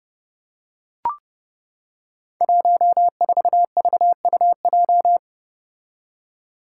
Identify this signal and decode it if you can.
Morse code, heard as a keyed tone, about 30 wpm, 700 Hz. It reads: 14VUJ
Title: Five Challenging Letter-Number Combinations 30wpm